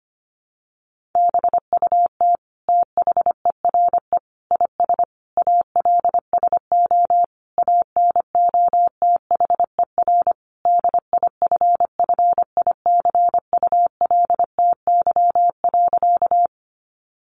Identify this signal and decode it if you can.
Morse code, heard as a keyed tone, about 25 wpm, 700 Hz. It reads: BUT THERE IS ALSO ANOTHER DIFFICULTY.
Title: Sentences from Top 1500 Words 25wpm